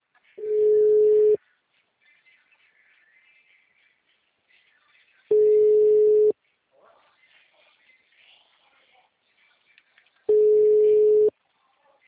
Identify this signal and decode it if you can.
Ringback tone (Europe, 425 Hz).
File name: out- -2057- - - .25913